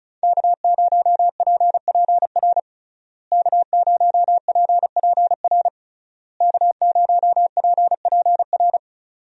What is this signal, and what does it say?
Morse code, heard as a keyed tone, about 36 wpm, 700 Hz. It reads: K0PPR K0PPR K0PPR
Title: USA Call Signs - 5 Characters - Repeated 3x - 35wpm